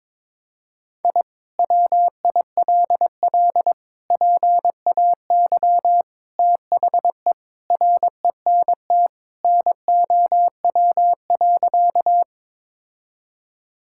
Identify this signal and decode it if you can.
Morse code, heard as a keyed tone, about 22 wpm, 700 Hz. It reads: I WILL PAY THE RENT NOW.
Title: Sentences from Top 1000 Words - Review Part 5 - 22wpm